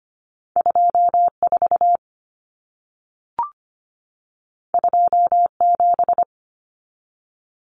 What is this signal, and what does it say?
Morse code, heard as a keyed tone, about 25 wpm, 700 Hz. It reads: 24 27
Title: Numbers - 2-Digit 25wpm